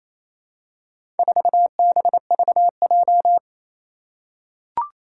Morse code, heard as a keyed tone, about 28 wpm, 700 Hz. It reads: 4BVJ
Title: Four Challenging Letter-Number Combinations 28wpm